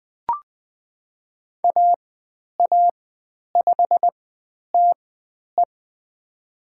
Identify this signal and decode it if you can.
Morse code, heard as a keyed tone, about 20 wpm, spaced as for 10 wpm, 700 Hz. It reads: AA5TE
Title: 102 Lesson 17g - Letter u - Call Signs 20wpm